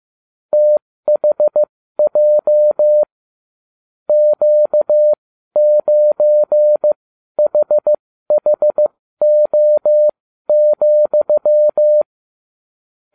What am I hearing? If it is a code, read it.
Morse code, heard as a keyed tone, about 15 wpm, spaced as for 12 wpm, 600 Hz. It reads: THJ Q9HHO,